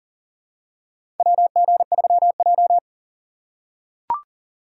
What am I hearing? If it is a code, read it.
Morse code, heard as a keyed tone, about 39 wpm, 700 Hz. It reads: WG3J
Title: USA Call Signs - 4 Characters - 3x Character Spacing - 40wpm